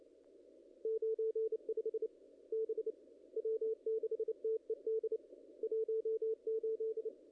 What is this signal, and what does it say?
Morse code, heard as a keyed tone, about 28 wpm, 440 Hz. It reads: 95 B W6TED 18
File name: contests / ss-cw-k5tr-op-n5rz -0000